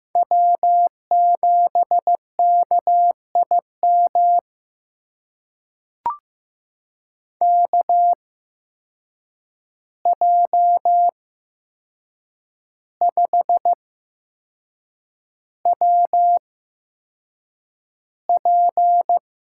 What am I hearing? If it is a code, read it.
Morse code, heard as a keyed tone, about 15 wpm, spaced as for 6 wpm, 700 Hz. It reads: W7KIM KJ5WP
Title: USA Call Signs - 5 Characters - 8x Character Spacing - 15wpm